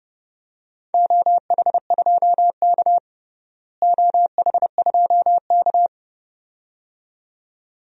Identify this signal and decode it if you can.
Morse code, heard as a keyed tone, about 32 wpm, 700 Hz. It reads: OH2K OH2K
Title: DXCC-100 Call Signs - 4 Characters - Repeated 2x - 30wpm